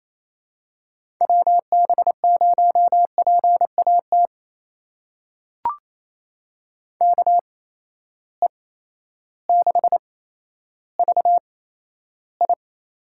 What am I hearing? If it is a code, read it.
Morse code, heard as a keyed tone, about 28 wpm, spaced as for 13 wpm, 700 Hz. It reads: WB0PAT KE6VI
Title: USA Call Signs - 8x Character Spacing - 28wpm